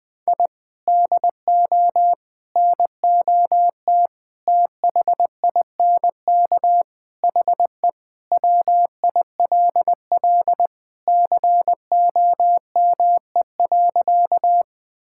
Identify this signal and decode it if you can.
Morse code, heard as a keyed tone, about 20 wpm, 700 Hz. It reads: I DO NOT THINK HE WILL COME.
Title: Sentences with 5-Word Phrases - Spoken Beginning - 20wpm